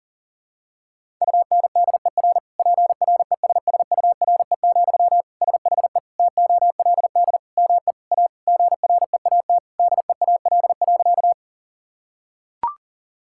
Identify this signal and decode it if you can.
Morse code, heard as a keyed tone, about 40 wpm, 700 Hz. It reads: UNDER PRESSURE, SHE TOLD ME A GREAT DEAL.
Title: Sentences from Top 1000 Words - Review Part 6 - 40wpm